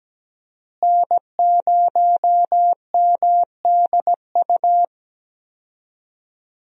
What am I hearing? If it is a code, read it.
Morse code, heard as a keyed tone, about 17 wpm, 700 Hz. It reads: N0MDU
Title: USA Call Signs - 5 Characters - Repeated 3x - 17wpm